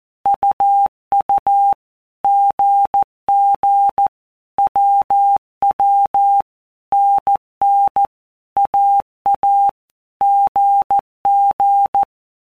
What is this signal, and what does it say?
Morse code, heard as a keyed tone, about 14 wpm, 800 Hz. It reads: UU GG WW NN AA GG
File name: Basic Letter Groups By Blocks / Groups@13 / AWUNDG@13